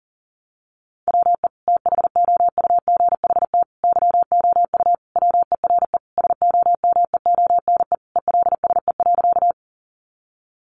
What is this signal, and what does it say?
Morse code, heard as a keyed tone, about 40 wpm, 700 Hz. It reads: WE THOUGHT YOU WERE SOMEONE ELSE.
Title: Sentences from Top 800 Words 40wpm